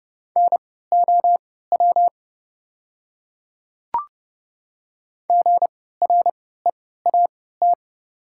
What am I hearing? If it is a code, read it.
Morse code, heard as a keyed tone, about 30 wpm, spaced as for 17 wpm, 700 Hz. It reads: NOW GREAT
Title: 3x Character Spacing - Top 500 Words - 30wpm